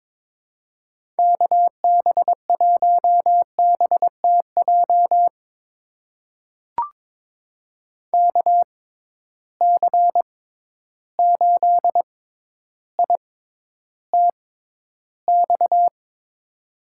Morse code, heard as a keyed tone, about 23 wpm, spaced as for 11 wpm, 700 Hz. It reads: KB1BTJ KC8ITX
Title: USA Call Signs - 6 Characters - 6x Character Spacing - 22wpm